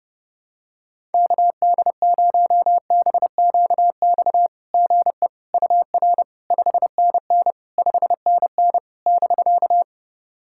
Morse code, heard as a keyed tone, about 32 wpm, 700 Hz. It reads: KD0BQX GE UR 5NN 5NN <BK>
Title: SOTA QSO Practice 30wpm